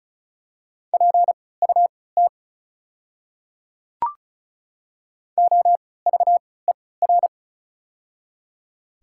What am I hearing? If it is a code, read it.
Morse code, heard as a keyed tone, about 36 wpm, spaced as for 20 wpm, 700 Hz. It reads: PUT OVER
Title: 3x Character Spacing - Top 200 Words - 35wpm